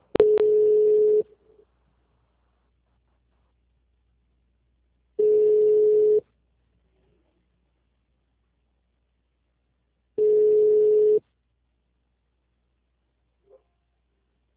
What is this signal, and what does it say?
Ringback tone (Europe, 425 Hz).